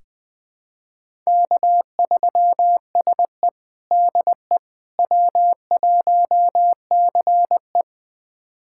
Morse code, heard as a keyed tone, about 21 wpm, 700 Hz. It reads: K3SE DE W1CE
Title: USA Callsign DE Callsign - 4 Characters - 20wpm